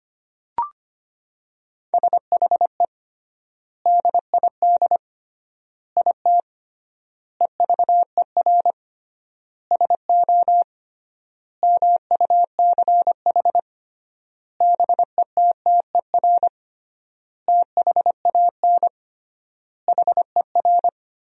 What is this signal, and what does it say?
Morse code, heard as a keyed tone, about 25 wpm, 700 Hz. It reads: SHE DID IT EVER SO MUCH BETTER THAN HER
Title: 3x Word-Spacing - Sentences from Top 400 Words - 25wpm